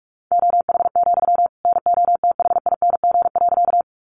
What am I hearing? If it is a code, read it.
Morse code, heard as a keyed tone, about 44 wpm, 700 Hz. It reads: OH, NOTHING.
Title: Sentences from Top 500 Words 45wpm